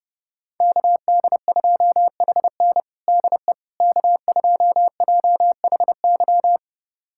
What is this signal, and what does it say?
Morse code, heard as a keyed tone, about 30 wpm, 700 Hz. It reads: KD2HN DE K2JHY
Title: USA Callsign DE Callsign - 5 Characters - 30wpm